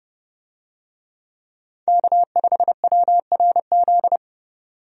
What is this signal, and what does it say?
Morse code, heard as a keyed tone, about 30 wpm, 700 Hz. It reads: K5WRZ
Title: USA Call Signs - 5 Characters - 30wpm